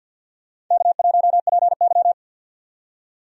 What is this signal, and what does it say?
Morse code, heard as a keyed tone, about 53 wpm, 700 Hz. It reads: K1PY
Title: USA Call Signs - 4 Characters - 50wpm